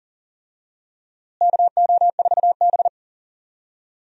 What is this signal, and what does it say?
Morse code, heard as a keyed tone, about 39 wpm, 700 Hz. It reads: KO4B
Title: USA Call Signs - 4 Characters - Repeated 3x - 40wpm